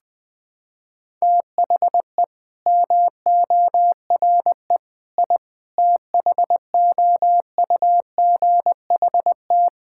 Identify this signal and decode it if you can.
Morse code, heard as a keyed tone, about 21 wpm, 700 Hz. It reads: THE MORE I THOUGHT
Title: Sentences with 8-Word Phrases - 20wpm